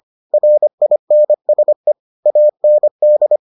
Morse code, heard as a keyed tone, about 25 wpm, 600 Hz. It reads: RINSE AND